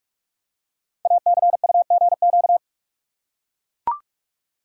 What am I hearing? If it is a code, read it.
Morse code, heard as a keyed tone, about 44 wpm, 700 Hz. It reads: ACUGQ
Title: Code Groups - 5 Letters - 45wpm